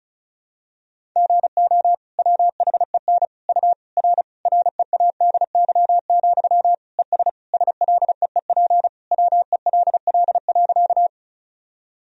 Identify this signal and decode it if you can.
Morse code, heard as a keyed tone, about 36 wpm, 700 Hz. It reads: GO WHEN U R READY, ES SLEEP WELL.
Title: Sentences from Top 500 Words and Abbreviations 35wpm